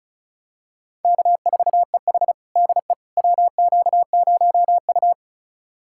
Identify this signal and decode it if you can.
Morse code, heard as a keyed tone, about 36 wpm, 700 Hz. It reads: K4EH DE WQ0U
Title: USA Callsign DE Callsign - 4 Characters - 35wpm